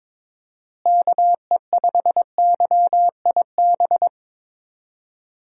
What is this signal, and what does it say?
Morse code, heard as a keyed tone, about 22 wpm, 700 Hz. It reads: KE5YIB
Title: USA Call Signs - 6 Characters - Repeated 3x - 22wpm